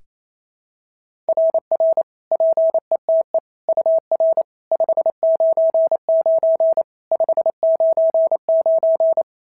Morse code, heard as a keyed tone, about 27 wpm, 650 Hz. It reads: RR PETE UR 599 599
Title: SOTA QSO Practice 28wpm